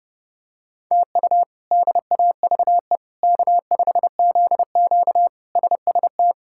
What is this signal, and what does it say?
Morse code, heard as a keyed tone, about 30 wpm, 700 Hz. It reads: TU DAVE K5ZQ SST
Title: K1USN SST QSO Practice - 30wpm